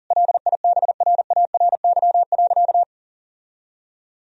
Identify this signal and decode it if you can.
Morse code, heard as a keyed tone, about 40 wpm, 700 Hz. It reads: LIBRARY.
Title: Sentences from Top 1000 Words - Review Part 4 - 40wpm